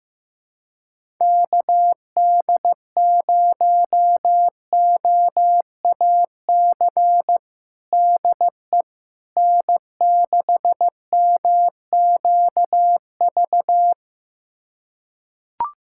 Morse code, heard as a keyed tone, about 15 wpm, 700 Hz. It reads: KD0OAC DE N6MQV
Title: USA Callsign DE Callsign 15wpm